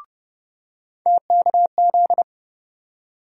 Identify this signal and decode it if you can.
Morse code, heard as a keyed tone, about 30 wpm, 700 Hz. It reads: TKZ
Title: USA Call Sign Postfixes - 3 Letters 30wpm